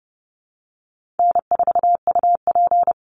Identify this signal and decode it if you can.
Morse code, heard as a keyed tone, about 30 wpm, 700 Hz. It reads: N4UP
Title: Call Signs Repeated 2x - 30wpm